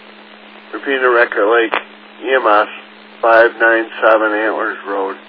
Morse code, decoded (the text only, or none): none